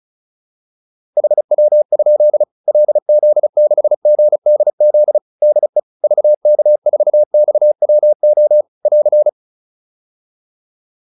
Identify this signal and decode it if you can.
Morse code, heard as a keyed tone, about 34 wpm, 600 Hz. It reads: HW? LZ6GDZ DE VK4XWO <AR>
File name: QSO407 35